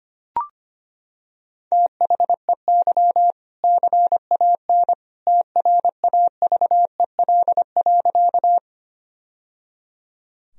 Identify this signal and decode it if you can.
Morse code, heard as a keyed tone, about 25 wpm, 700 Hz. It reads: THEY CAN TRAVEL.
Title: Sentences from Top 1000 Words - Review Part 2 - 25wpm